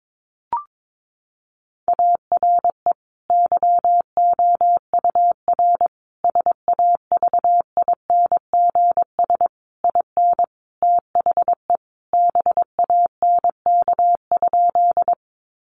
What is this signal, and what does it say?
Morse code, heard as a keyed tone, about 22 wpm, 700 Hz. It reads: ARE YOUR SAVINGS IN THE BANK?
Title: Sentences from Top 1200 Words 22wpm